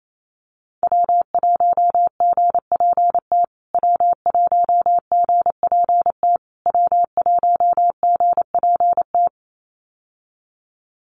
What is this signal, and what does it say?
Morse code, heard as a keyed tone, about 28 wpm, 700 Hz. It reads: W1GPT W1GPT W1GPT
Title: USA Call Signs Repeated 3x 28wpm